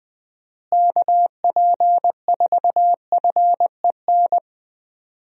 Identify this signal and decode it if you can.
Morse code, heard as a keyed tone, about 20 wpm, 700 Hz. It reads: KP4FEN
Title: DXCC-100 Call Signs - 6 Characters - Repeated 2x - 20wpm